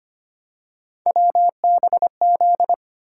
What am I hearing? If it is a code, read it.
Morse code, heard as a keyed tone, about 25 wpm, 700 Hz. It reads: WBZ
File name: Three Letter-Number Combinations - 25wpm